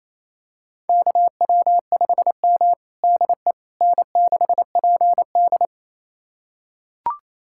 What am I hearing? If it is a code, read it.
Morse code, heard as a keyed tone, about 28 wpm, 700 Hz. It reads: KW5M DE N6PD
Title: USA Callsign DE Callsign - 4 Characters - 28wpm